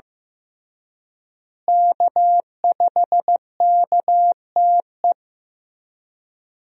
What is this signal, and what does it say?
Morse code, heard as a keyed tone, about 15 wpm, 700 Hz. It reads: K5KTE
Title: USA Call Signs - 5 Characters - Repeated 3x - 15wpm